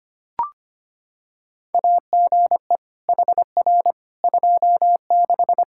Morse code, heard as a keyed tone, about 25 wpm, 700 Hz. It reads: AGE HR 26
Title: Practice QSO - Exchange 3 - More Info - 25wpm